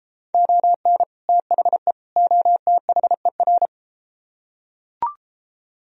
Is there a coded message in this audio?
Yes: Morse code from 0.3 to 3.7 s, heard as a keyed tone, about 33 wpm, 700 Hz. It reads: ON THE OTHER